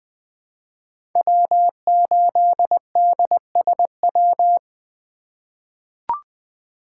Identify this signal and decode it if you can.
Morse code, heard as a keyed tone, about 20 wpm, 700 Hz. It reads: W8DSW
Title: USA Call Signs - 4x Character Spacing - 20wpm